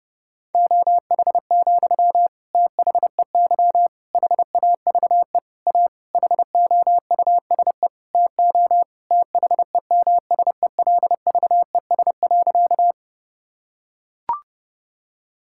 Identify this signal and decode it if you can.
Morse code, heard as a keyed tone, about 32 wpm, 700 Hz. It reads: OH, THEY HAVE A HOUSE TO THEMSELVES.
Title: Sentences from Top 1000 Words - Review Part 5 - 30wpm